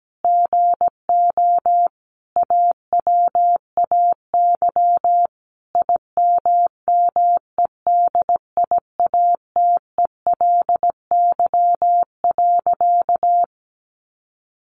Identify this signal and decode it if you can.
Morse code, heard as a keyed tone, about 17 wpm, 700 Hz. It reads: GO AWAY IMMEDIATELY.
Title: Sentences from Top 1100 Words 17wpm